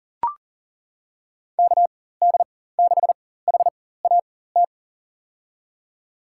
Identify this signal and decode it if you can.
Morse code, heard as a keyed tone, about 40 wpm, spaced as for 19 wpm, 700 Hz. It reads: KD6HAT
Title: USA Call Signs - 4x Character Spacing - 40wpm